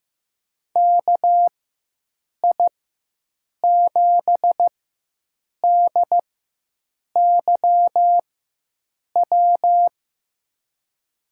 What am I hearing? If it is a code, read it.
Morse code, heard as a keyed tone, about 15 wpm, spaced as for 7 wpm, 700 Hz. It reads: KI7DYW
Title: USA Call Signs - 6 Characters - 4x Character Spacing - 15wpm